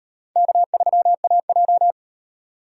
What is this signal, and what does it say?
Morse code, heard as a keyed tone, about 38 wpm, 700 Hz. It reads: K3AJ